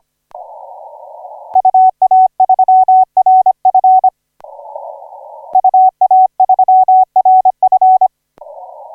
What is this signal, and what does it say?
Morse code, heard as a keyed tone, about 25 wpm, 750 Hz. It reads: UA3RF UA3RF
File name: 8Q7CF 1817UTC 17m